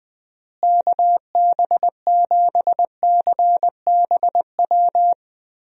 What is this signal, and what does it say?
Morse code, heard as a keyed tone, about 20 wpm, 700 Hz. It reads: KB7CBW
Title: USA Call Signs - 6 Characters - Repeated 3x - 20wpm